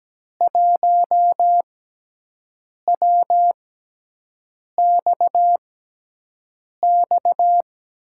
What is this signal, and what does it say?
Morse code, heard as a keyed tone, about 17 wpm, spaced as for 6 wpm, 700 Hz. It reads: 1WXX
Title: USA Call Signs - 5 Characters - 6x Character Spacing - 17wpm